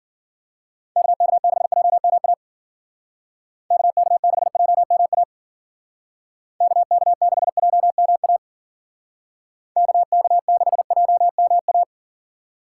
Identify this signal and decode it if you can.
Morse code, heard as a keyed tone, about 52 wpm, 700 Hz. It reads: KK6JMA KK6JMA KK6JMA KK6JMA